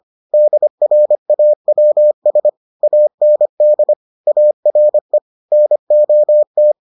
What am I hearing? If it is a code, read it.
Morse code, heard as a keyed tone, about 25 wpm, 600 Hz. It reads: DRAWS AND ARE NOT